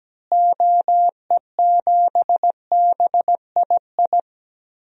Morse code, heard as a keyed tone, about 17 wpm, 700 Hz. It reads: OE7BII
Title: DXCC-100 Call Signs - 6 Characters - Repeated 3x - 17wpm